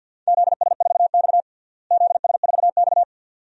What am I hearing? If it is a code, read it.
Morse code, heard as a keyed tone, about 50 wpm, 700 Hz. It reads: ZS4X ZS4X
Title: DXCC-100 Call Signs - 4 Characters - Repeated 2x - 50wpm